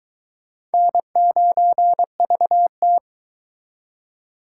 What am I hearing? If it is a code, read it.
Morse code, heard as a keyed tone, about 22 wpm, 700 Hz. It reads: N9VT